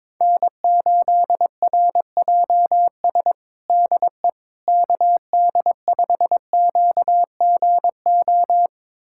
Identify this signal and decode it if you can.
Morse code, heard as a keyed tone, about 22 wpm, 700 Hz. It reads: N8RJS DE KD5QGO
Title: USA Callsign DE Callsign 22wpm